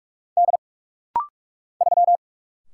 Morse code, heard as a keyed tone, about 44 wpm, 700 Hz. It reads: D 3
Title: Single Letter-Number - Rapid-Fire 45wpm